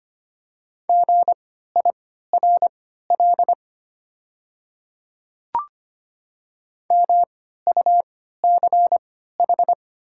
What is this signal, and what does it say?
Morse code, heard as a keyed tone, about 25 wpm, spaced as for 14 wpm, 700 Hz. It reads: GIRL MUCH
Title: 3x Character Spacing - Top 500 Words - 25wpm